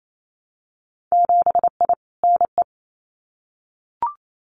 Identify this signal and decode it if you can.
Morse code, heard as a keyed tone, about 28 wpm, 700 Hz. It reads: 7I NE
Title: Winter Field Day Exchanges Encore 28wpm